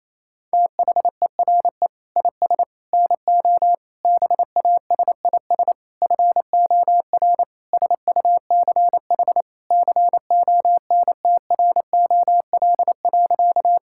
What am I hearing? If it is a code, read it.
Morse code, heard as a keyed tone, about 27 wpm, 700 Hz. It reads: THERE IS NO BASIS FOR SUCH CONTROL.
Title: Sentences from Top 1000 Words - Review Part 5 - 28wpm